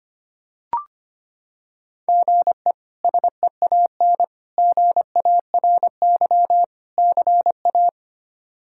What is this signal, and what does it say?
Morse code, heard as a keyed tone, about 25 wpm, 700 Hz. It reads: GE SEAN GARY CA
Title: K1USN SST QSO Practice - 25wpm